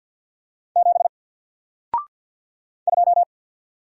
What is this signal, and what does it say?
Morse code, heard as a keyed tone, about 48 wpm, 700 Hz. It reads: 7 2
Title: Single Character (Letter:Number:Punctuation:Prosign) Alt 50wpm